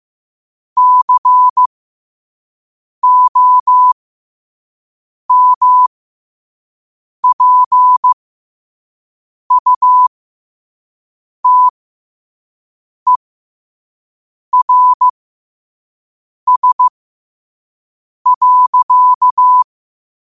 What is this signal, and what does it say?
Morse code, heard as a keyed tone, about 15 wpm, spaced as for 5 wpm, 1000 Hz. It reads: COMPUTERS.
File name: qotdfiles / 5wpm